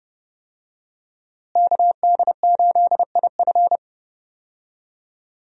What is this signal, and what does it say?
Morse code, heard as a keyed tone, about 30 wpm, 700 Hz. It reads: KD8IF
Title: USA Call Signs - 5 Characters - Repeated 2x - 30wpm